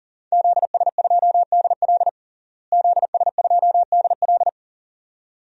Morse code, heard as a keyed tone, about 40 wpm, 700 Hz. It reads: ZS2DL ZS2DL
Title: DXCC-100 Call Signs - 5 Characters - Repeated 2x - 40wpm